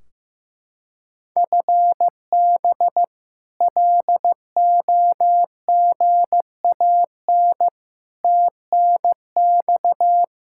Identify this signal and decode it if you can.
Morse code, heard as a keyed tone, about 15 wpm, 700 Hz. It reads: FB LOGAN TNX
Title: K1USN SST QSO Practice 15wpm